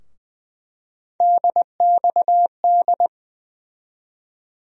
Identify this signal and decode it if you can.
Morse code, heard as a keyed tone, about 21 wpm, 700 Hz. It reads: DXD
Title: USA Call Sign Postfixes - 3 Letters 20wpm